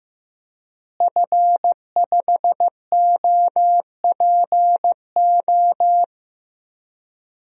Morse code, heard as a keyed tone, about 16 wpm, 700 Hz. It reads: F5OPO
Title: DXCC-100 Call Signs - 5 Characters - Repeated 3x - 15wpm